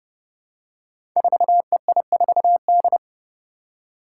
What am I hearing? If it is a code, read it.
Morse code, heard as a keyed tone, about 28 wpm, 700 Hz. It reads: 4EI4D
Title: Five Challenging Letter-Number Combinations 30wpm